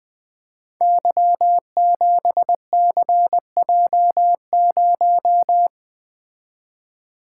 Morse code, heard as a keyed tone, about 20 wpm, 700 Hz. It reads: Y7CJ0
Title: Code Groups - 5 Letter:Numbers - 20wpm